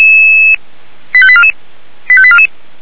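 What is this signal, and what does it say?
Five-tone selective call (ZVEI), heard as quick runs of five tones, tones of about 70 ms. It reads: 85844 85844
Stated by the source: ZVEI five-tone selective call received on an SDR.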